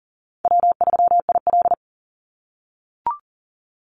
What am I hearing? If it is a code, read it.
Morse code, heard as a keyed tone, about 40 wpm, 700 Hz. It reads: W3IL
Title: Easy Call Signs - 40wpm